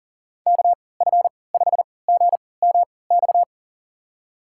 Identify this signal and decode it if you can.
Morse code, heard as a keyed tone, about 37 wpm, spaced as for 22 wpm, 700 Hz. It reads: KF5GMX
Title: USA Call Signs - 6 Characters - 3x Character Spacing - 40wpm